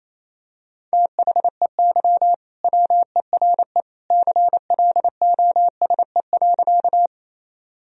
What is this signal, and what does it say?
Morse code, heard as a keyed tone, about 28 wpm, 700 Hz. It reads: THEY WERE CLOSE.
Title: Sentences from Top 1000 Words - Review Part 1 - 28wpm